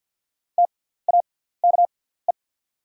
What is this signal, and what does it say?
Morse code, heard as a keyed tone, about 48 wpm, spaced as for 17 wpm, 690 Hz. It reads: TAKE
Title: 6x Character Spacing - Top 200 Words - 50wpm